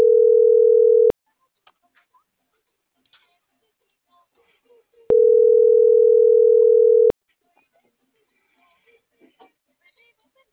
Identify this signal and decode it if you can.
Ringback tone (North America), 440 + 480 Hz.